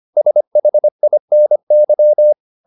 Morse code, heard as a keyed tone, about 25 wpm, 600 Hz. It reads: SHINY